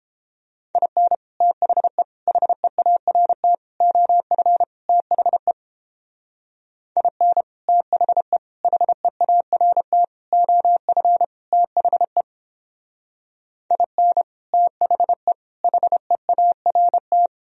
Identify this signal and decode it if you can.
Morse code, heard as a keyed tone, about 29 wpm, 700 Hz. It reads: IN THE HEART OF THE IN THE HEART OF THE IN THE HEART